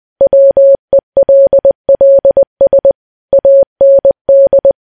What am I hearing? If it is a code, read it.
Morse code, heard as a keyed tone, about 20 wpm, 550 Hz. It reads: WELLS AND
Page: news-headlines-in-morse-code-at-20-wpm